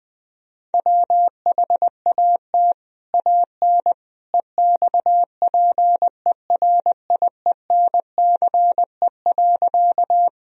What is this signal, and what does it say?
Morse code, heard as a keyed tone, about 21 wpm, 700 Hz. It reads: WHAT AN EXPERIENCE.
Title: Sentences from Top 1000 Words - Review Part 3 - 20wpm